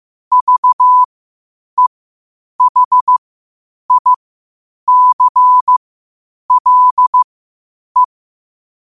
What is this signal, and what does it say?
Morse code, heard as a keyed tone, about 15 wpm, spaced as for 8 wpm, 1000 Hz. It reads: VEHICLE
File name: qotdfiles / 8wpm